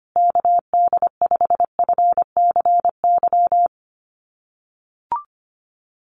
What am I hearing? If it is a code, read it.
Morse code, heard as a keyed tone, about 25 wpm, 700 Hz. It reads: KD5FCY
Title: USA Call Signs Repeated 2x 25wpm